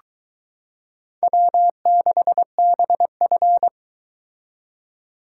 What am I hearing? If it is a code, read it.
Morse code, heard as a keyed tone, about 22 wpm, 700 Hz. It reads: W6BF